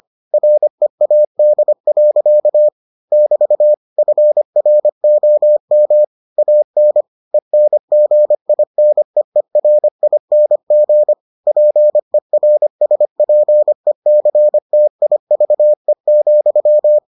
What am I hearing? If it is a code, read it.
Morse code, heard as a keyed tone, about 25 wpm, 600 Hz. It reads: READ. <BT> FROM AN ENGINEERING PERSPECTIVE,